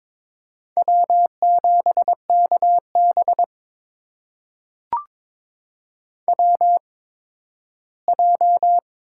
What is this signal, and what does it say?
Morse code, heard as a keyed tone, about 22 wpm, spaced as for 13 wpm, 700 Hz. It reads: W7KB WJ
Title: USA Call Signs - 4 Characters - 8x Character Spacing - 22wpm